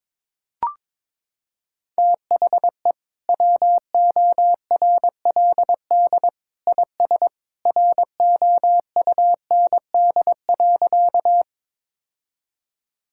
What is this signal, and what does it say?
Morse code, heard as a keyed tone, about 22 wpm, 700 Hz. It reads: THE WORLD IS ROUND.
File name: practice / 22wpm / Sentences with 2-Word Phrases - 22wpm